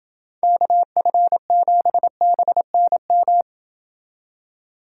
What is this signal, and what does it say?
Morse code, heard as a keyed tone, about 29 wpm, 700 Hz. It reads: KF7BNM